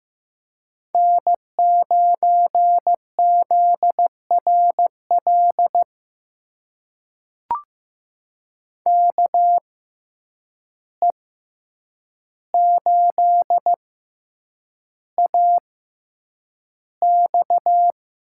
Morse code, heard as a keyed tone, about 15 wpm, spaced as for 8 wpm, 700 Hz. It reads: N9ZRL KE8AX
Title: USA Call Signs - 6x Character Spacing - 15wpm